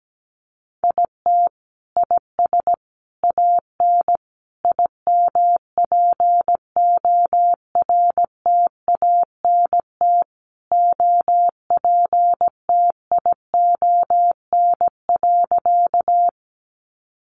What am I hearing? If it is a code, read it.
Morse code, heard as a keyed tone, about 17 wpm, 700 Hz. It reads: IT IS AN IMPORTANT OPTION.
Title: Sentences from Top 1300 Words 17wpm